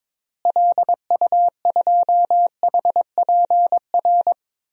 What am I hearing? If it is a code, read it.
Morse code, heard as a keyed tone, about 22 wpm, 700 Hz. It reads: LU2HPR
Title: DXCC-100 Call Signs - 6 Characters - Repeated 2x - 22wpm